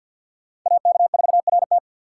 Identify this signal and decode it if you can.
Morse code, heard as a keyed tone, about 51 wpm, 690 Hz. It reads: AK4RT
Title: USA Call Signs - 2x Character Spacing - 50wpm